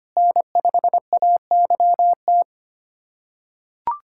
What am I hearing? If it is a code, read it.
Morse code, heard as a keyed tone, about 25 wpm, 700 Hz. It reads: N5AYT
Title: USA Callsign DE Callsign - 5 Characters - 25wpm